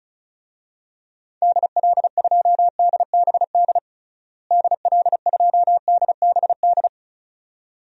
Morse code, heard as a keyed tone, about 34 wpm, 700 Hz. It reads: DL2DBD DL2DBD
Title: DXCC-100 Call Signs - 6 Characters - Repeated 2x - 35wpm